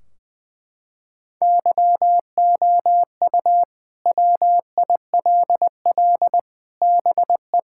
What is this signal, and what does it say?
Morse code, heard as a keyed tone, about 20 wpm, 700 Hz. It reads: YOU WILL BE
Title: Sentences from Top 1600 Words 20wpm